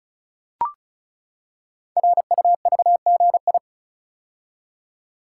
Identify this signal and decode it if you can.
Morse code, heard as a keyed tone, about 36 wpm, 690 Hz. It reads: RUVGI
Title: Code Groups - 5 Letter:Numbers - 35wpm